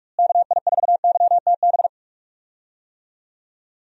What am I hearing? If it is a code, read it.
Morse code, heard as a keyed tone, about 46 wpm, 700 Hz. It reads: KI4YTB
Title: USA Call Signs - 6 Characters - Repeated 2x - 45wpm